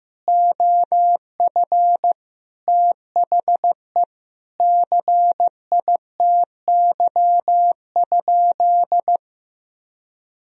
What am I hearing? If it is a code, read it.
Morse code, heard as a keyed tone, about 15 wpm, 700 Hz. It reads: OF THE CITY?
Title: Sentences from Top 1000 Words - Review Part 2 - 15wpm